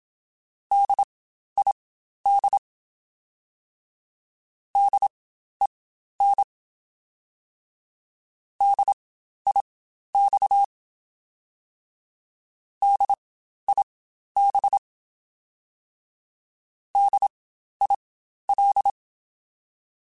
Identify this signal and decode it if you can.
Morse code, heard as a keyed tone, about 27 wpm, spaced as for 12 wpm, 800 Hz. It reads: DID DEN DIX DIB DIL
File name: Spaced Alphas / Spaced Alphas mp3s / Ds sp @25